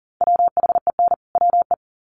Morse code, heard as a keyed tone, about 40 wpm, 700 Hz. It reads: WHEN WE
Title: Sets of 2 Words 40wpm